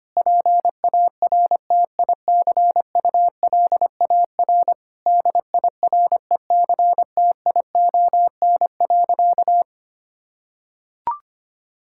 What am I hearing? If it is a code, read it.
Morse code, heard as a keyed tone, about 25 wpm, 700 Hz. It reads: PARTICULAR DIRECTION.
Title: Sentences from Top 1000 Words - Review Part 5 - 25wpm